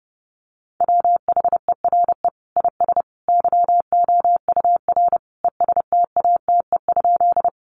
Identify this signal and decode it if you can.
Morse code, heard as a keyed tone, about 30 wpm, 700 Hz. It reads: WHERE IS YOUR ESTATE?
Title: Sentences from Top 1200 Words 30wpm